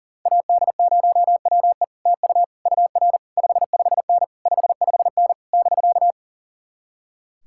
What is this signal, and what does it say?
Morse code, heard as a keyed tone, about 53 wpm, spaced as for 46 wpm, 700 Hz. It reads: AD0WE TU UR 55N 55N <BK>
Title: Speed-Racing Common QSO Phrases - 40wpm